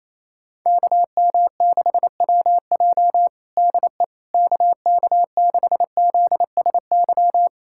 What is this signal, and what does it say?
Morse code, heard as a keyed tone, about 28 wpm, 700 Hz. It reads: KM6WJ DE KK6ZSY
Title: USA Callsign DE Callsign 28wpm